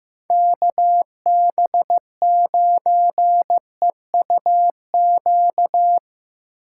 Morse code, heard as a keyed tone, about 15 wpm, 700 Hz. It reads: KB9EUQ
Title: USA Call Signs - 6 Characters - 8x Character Spacing - 15wpm